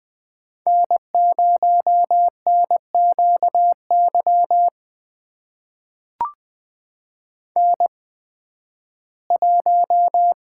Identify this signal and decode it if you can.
Morse code, heard as a keyed tone, about 20 wpm, spaced as for 13 wpm, 700 Hz. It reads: N0NQY N1
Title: USA Call Signs - 5 Characters - 8x Character Spacing - 20wpm